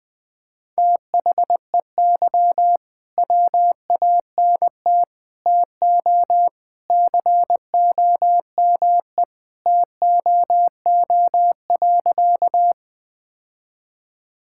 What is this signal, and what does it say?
Morse code, heard as a keyed tone, about 21 wpm, 700 Hz. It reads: THEY WANT TO COME TOO.
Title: Sentences with 3-Word Phrases - 20wpm